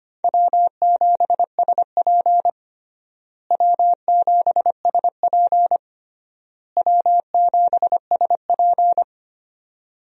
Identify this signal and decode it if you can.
Morse code, heard as a keyed tone, about 25 wpm, 700 Hz. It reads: W7SP W7SP W7SP
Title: USA Call Signs - 4 Characters - Repeated 3x - 25wpm